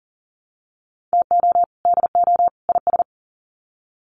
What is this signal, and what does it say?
Morse code, heard as a keyed tone, about 40 wpm, 700 Hz. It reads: TO DO IS
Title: Sets of 3 Words 40wpm